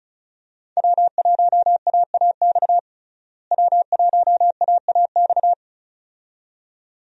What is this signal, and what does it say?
Morse code, heard as a keyed tone, about 36 wpm, 700 Hz. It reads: W1AAX W1AAX
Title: USA Call Signs - 5 Characters - Repeated 2x - 35wpm